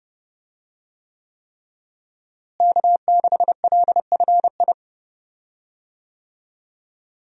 Morse code, heard as a keyed tone, about 32 wpm, 700 Hz. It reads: K6LFI
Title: USA Call Signs - 5 Characters - Repeated 3x - 30wpm